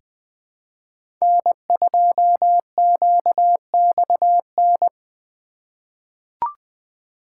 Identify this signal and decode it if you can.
Morse code, heard as a keyed tone, about 20 wpm, 700 Hz. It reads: N2QXN
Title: USA Call Signs - 5 Characters - 2x Character Spacing - 20wpm